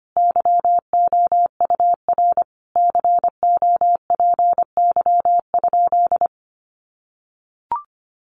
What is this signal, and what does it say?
Morse code, heard as a keyed tone, about 25 wpm, 700 Hz. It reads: YOUR COPY?
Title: Sentences from Top 1400 Words 25wpm